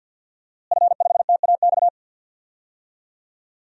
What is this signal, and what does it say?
Morse code, heard as a keyed tone, about 45 wpm, 700 Hz. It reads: F5TAX
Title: DXCC-100 Call Signs - 5 Characters - Repeated 2x - 50wpm